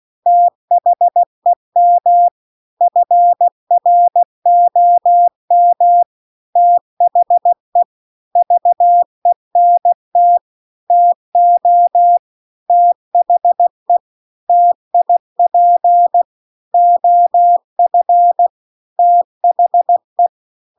Morse code, heard as a keyed tone, about 16 wpm, 700 Hz. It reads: THEM FROM THE VENT TO THE TIP OF THE
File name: CW-The-Big-Twoheaded-River-2-16WPM